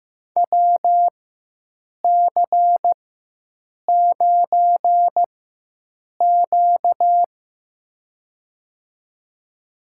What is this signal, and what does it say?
Morse code, heard as a keyed tone, about 16 wpm, spaced as for 7 wpm, 700 Hz. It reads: WC9Q
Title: USA Call Signs - 4 Characters - 4x Character Spacing - 15wpm